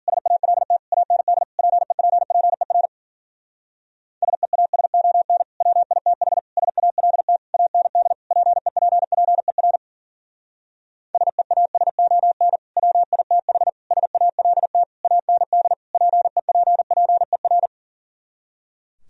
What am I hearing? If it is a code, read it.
Morse code, heard as a keyed tone, about 51 wpm, 700 Hz. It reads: SALT AND PEPPER SEASON WITH SALT AND PEPPER SEASON WITH SALT AND PEPPER